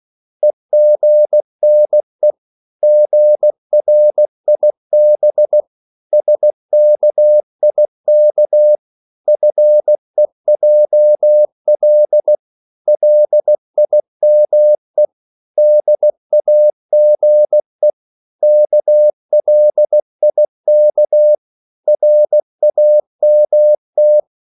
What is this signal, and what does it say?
Morse code, heard as a keyed tone, about 16 wpm, 600 Hz. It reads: EGNE GRIB SKIK FEJL LIME DAGE KLIK RAMT
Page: Korte danske ord 16wpm